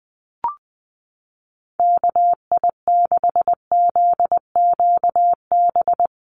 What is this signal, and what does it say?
Morse code, heard as a keyed tone, about 19 wpm, 700 Hz. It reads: KI6ZQB
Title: USA Call Signs Repeated 2x 20wpm